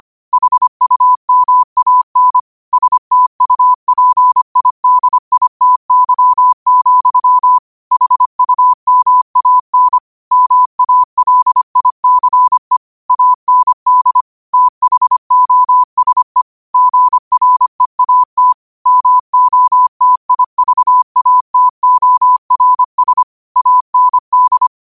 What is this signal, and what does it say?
Morse code, heard as a keyed tone, about 25 wpm, 1000 Hz. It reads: HUMAN STUPIDITY, HUMAN MALICE AND THOSE GREAT MOTIVATORS AND
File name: qotdfiles / 25wpm